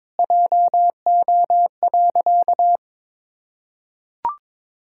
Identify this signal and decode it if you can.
Morse code, heard as a keyed tone, about 22 wpm, 700 Hz. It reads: JO.
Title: Pangram Challenge v2 - 22wpm